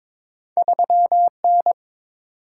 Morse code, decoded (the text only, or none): 3N